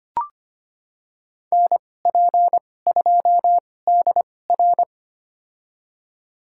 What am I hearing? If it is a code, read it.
Morse code, heard as a keyed tone, about 25 wpm, spaced as for 18 wpm, 700 Hz. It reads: NP2DR
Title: USA Call Signs - 5 Characters - 2x Character Spacing - 25wpm